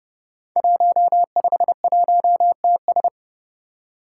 Morse code, heard as a keyed tone, about 30 wpm, 700 Hz. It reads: 151TS
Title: Five Challenging Letter-Number Combinations 30wpm